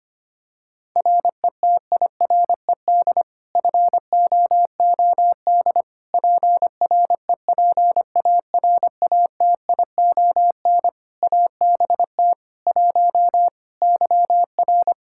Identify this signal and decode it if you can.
Morse code, heard as a keyed tone, about 25 wpm, 700 Hz. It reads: RETIRED FOOD PREPARATION ABT 1 YR
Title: Practice QSO - Exchange 3 - More Info - 25wpm